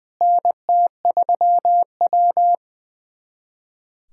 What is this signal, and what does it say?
Morse code, heard as a keyed tone, about 20 wpm, 700 Hz. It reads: NT3W
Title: USA Callsign DE Callsign - 4 Characters - 20wpm